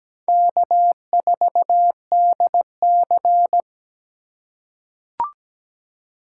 Morse code, heard as a keyed tone, about 17 wpm, 700 Hz. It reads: K4DC
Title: USA Call Signs - 2x Character Spacing - 17wpm